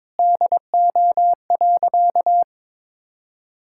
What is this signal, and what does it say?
Morse code, heard as a keyed tone, about 22 wpm, 700 Hz. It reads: DO.
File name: practice / 22wpm / Sentences with 8-Word Phrases - 22wpm